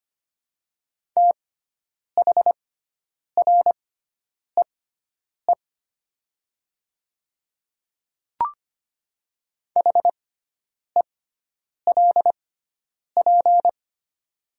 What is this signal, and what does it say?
Morse code, heard as a keyed tone, about 25 wpm, spaced as for 9 wpm, 700 Hz. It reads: THREE HELP
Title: 6x Character Spacing - Top 200 Words - 25wpm